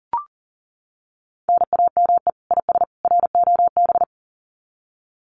Morse code, heard as a keyed tone, about 40 wpm, 700 Hz. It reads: NAME IS ROB
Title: Common QSO Phrases 40wpm